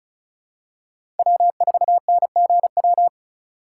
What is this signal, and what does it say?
Morse code, heard as a keyed tone, about 34 wpm, 700 Hz. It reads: W4NGW
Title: USA Call Signs - 5 Characters - 4x Character Spacing - 35wpm